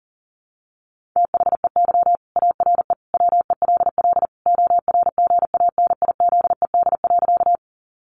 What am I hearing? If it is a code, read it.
Morse code, heard as a keyed tone, about 40 wpm, 700 Hz. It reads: THEY ARE WELL ORGANIZED.
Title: Sentences from Top 1300 Words 40wpm